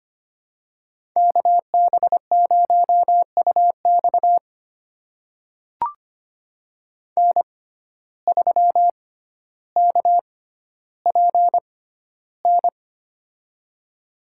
Morse code, heard as a keyed tone, about 25 wpm, spaced as for 13 wpm, 700 Hz. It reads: KB0UX N3KPN
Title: USA Call Signs - 5 Characters - 6x Character Spacing - 25wpm